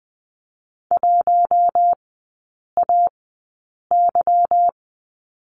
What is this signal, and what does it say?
Morse code, heard as a keyed tone, about 20 wpm, spaced as for 8 wpm, 690 Hz. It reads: 1AY
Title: 159 Lesson 26e - Letter g - Three Characters 20wpm